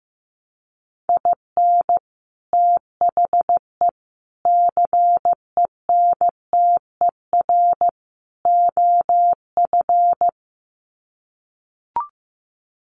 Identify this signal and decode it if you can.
Morse code, heard as a keyed tone, about 15 wpm, 700 Hz. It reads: IN THE CENTER OF
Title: Sets of 4 Words 15wpm